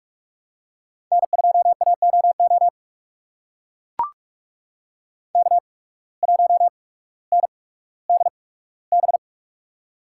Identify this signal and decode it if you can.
Morse code, heard as a keyed tone, about 44 wpm, spaced as for 19 wpm, 700 Hz. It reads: N2AOO K1NDB
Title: USA Call Signs - 5 Characters - 8x Character Spacing - 45wpm